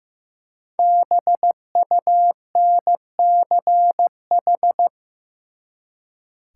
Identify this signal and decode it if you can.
Morse code, heard as a keyed tone, about 15 wpm, 700 Hz. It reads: BUNCH
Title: Five Letter Words - 15wpm